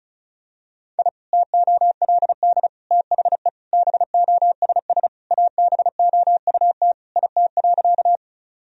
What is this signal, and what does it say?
Morse code, heard as a keyed tone, about 36 wpm, 700 Hz. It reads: I TOLD THE BOSS ABOUT IT.
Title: Sentences from Top 1000 Words - Review Part 4 - 35wpm